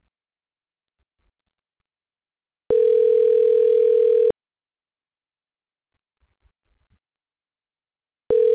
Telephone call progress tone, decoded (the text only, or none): ringback tone